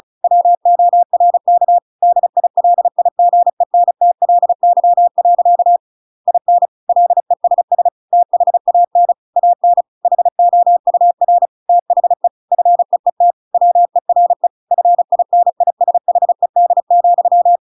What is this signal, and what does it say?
Morse code, heard as a keyed tone, about 34 wpm, 700 Hz. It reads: WORK DILIGENTLY. IN LESS THAN AN HOUR THE FEET WERE FINISHED,
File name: Pinocchio / Pinocchio ch8 35 35 ik1hge